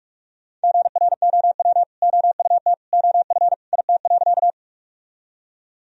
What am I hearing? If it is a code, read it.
Morse code, heard as a keyed tone, about 46 wpm, 700 Hz. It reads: GROW OUT OF IT.
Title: Sentences from Top 1000 Words - Review Part 1 - 45wpm